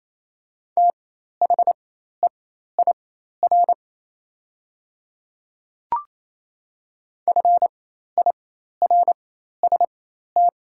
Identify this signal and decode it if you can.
Morse code, heard as a keyed tone, about 28 wpm, spaced as for 13 wpm, 700 Hz. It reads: THEIR FIRST
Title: 4x Character Spacing - Top 100 Words - 28wpm